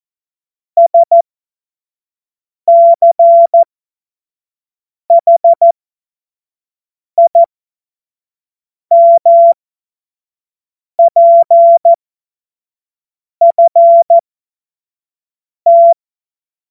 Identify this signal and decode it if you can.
Morse code, heard as a keyed tone, about 14 wpm, spaced as for 5 wpm, 680 Hz. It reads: SCHIMPFT
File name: Morsekurs / Witze / Witz 15. November 2022 25 F70